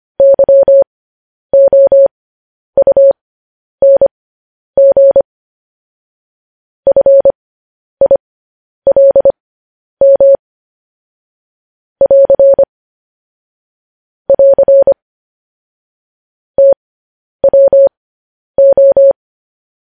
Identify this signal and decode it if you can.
Morse code, heard as a keyed tone, about 25 wpm, spaced as for 10 wpm, 550 Hz. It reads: YOUNG FILM <AR> <AR> TWO
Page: news headlines in morse code at 10 wpm